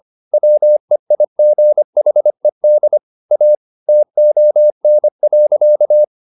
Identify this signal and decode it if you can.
Morse code, heard as a keyed tone, about 25 wpm, 600 Hz. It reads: WEIGHED A TON.
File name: the old man and the sea part 1 25wpm